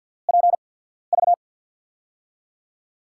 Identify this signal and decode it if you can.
Morse code, heard as a keyed tone, about 47 wpm, spaced as for 13 wpm, 700 Hz. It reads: PV
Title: USA Call Signs - 6 Characters - 8x Character Spacing - 50wpm